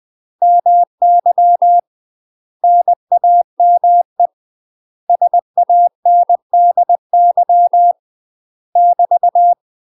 Morse code, heard as a keyed tone, about 20 wpm, 700 Hz. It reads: MY NAME SANDY <BT>
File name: UK MASTER QSOs TRAINING / 20WPM / UK MASTER QSOs 0013